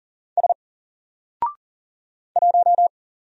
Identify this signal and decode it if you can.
Morse code, heard as a keyed tone, about 40 wpm, 700 Hz. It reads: S 1
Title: Single Character (Letter:Number:Punctuation:Prosign) Alt 40wpm